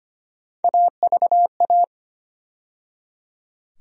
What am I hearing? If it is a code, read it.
Morse code, heard as a keyed tone, about 25 wpm, 700 Hz. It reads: AVA
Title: USA Call Sign Postfixes - 3 Letters 25wpm